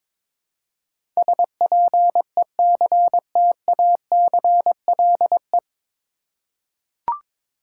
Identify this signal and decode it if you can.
Morse code, heard as a keyed tone, about 22 wpm, 700 Hz. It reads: SPECTACLE
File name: 9 Letter Words - First 3 Letters Given - 22wpm